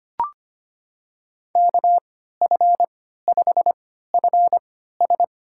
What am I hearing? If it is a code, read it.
Morse code, heard as a keyed tone, about 25 wpm, spaced as for 14 wpm, 700 Hz. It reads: KF5FS
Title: USA Call Signs - 5 Characters - 3x Character Spacing - 25wpm